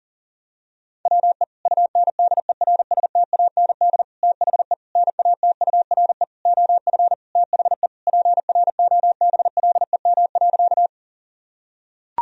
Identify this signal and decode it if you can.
Morse code, heard as a keyed tone, about 40 wpm, 700 Hz. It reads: WE UNDERSTAND THE NATURE OF THE PROBLEM.
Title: Sentences from Top 1000 Words - Review Part 2 - 40wpm